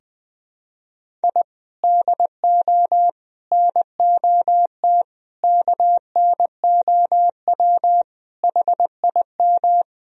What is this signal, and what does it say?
Morse code, heard as a keyed tone, about 20 wpm, 700 Hz. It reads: I DO NOT KNOW HIM
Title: Sentences from Top 1000 Words - Review Part 1 - 20wpm